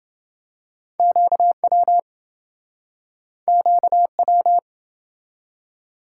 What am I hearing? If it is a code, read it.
Morse code, heard as a keyed tone, about 30 wpm, 700 Hz. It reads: QW QW